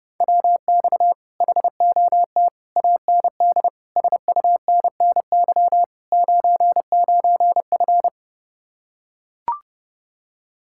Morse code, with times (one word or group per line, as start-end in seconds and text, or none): 0.2-1.1 WX
1.4-2.5 HOT
2.8-3.7 AND
4.0-5.8 SUNNY
6.1-8.1 99F